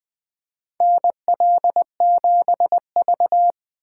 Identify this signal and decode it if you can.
Morse code, heard as a keyed tone, about 21 wpm, 700 Hz. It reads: NL7V
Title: K1USN SST QSO Practice - 20wpm